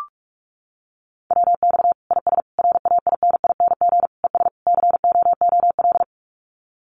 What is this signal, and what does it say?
Morse code, heard as a keyed tone, about 44 wpm, 700 Hz. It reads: WX IS RAINING ES COOL
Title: Common QSO Phrases 45wpm